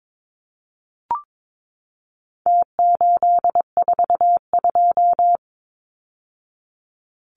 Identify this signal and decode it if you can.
Morse code, heard as a keyed tone, about 22 wpm, 700 Hz. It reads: T842
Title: Numbers - Contest Style - 4-Digit 22wpm